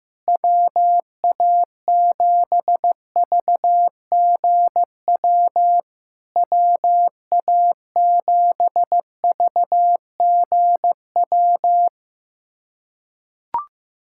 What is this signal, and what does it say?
Morse code, heard as a keyed tone, about 15 wpm, 700 Hz. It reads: WA7VGW WA7VGW
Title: SOTA QSO Practice 15wpm